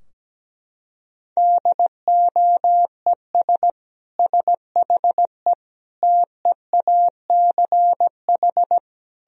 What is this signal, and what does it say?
Morse code, heard as a keyed tone, about 17 wpm, 700 Hz. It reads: DOES SHE TEACH
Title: Sentences from Top 1000 Words - Review Part 3 - 17wpm